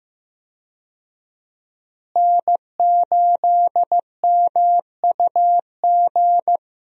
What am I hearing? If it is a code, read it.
Morse code, heard as a keyed tone, about 15 wpm, 700 Hz. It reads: N8MUG
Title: USA Call Signs - 5 Characters - Repeated 3x - 15wpm